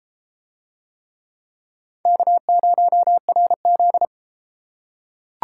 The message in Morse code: K0RZ

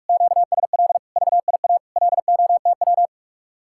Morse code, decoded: QSL VIA LOTW

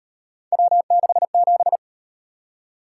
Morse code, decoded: W67

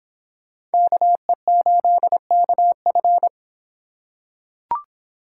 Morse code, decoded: KE8KF

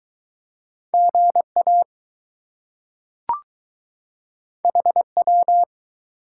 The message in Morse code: GA HW

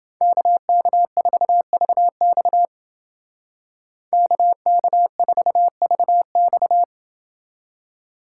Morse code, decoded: KK4VX KK4VX

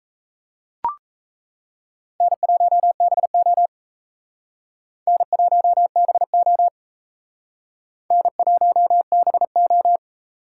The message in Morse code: N1BO N1BO N1BO

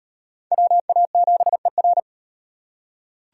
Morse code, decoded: WA7ER

